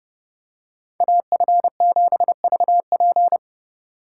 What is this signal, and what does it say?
Morse code, heard as a keyed tone, about 29 wpm, 700 Hz. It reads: AF7VP